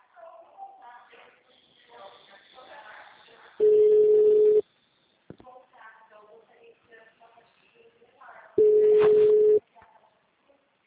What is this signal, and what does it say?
Ringback tone (Europe, 425 Hz), with speech.
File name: out- -2003- - - .1911